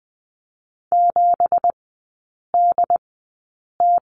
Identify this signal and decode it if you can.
Morse code, heard as a keyed tone, about 20 wpm, spaced as for 8 wpm, 700 Hz. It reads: 7DT
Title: 205 Lesson 33e - Letter j - Three Characters 20wpm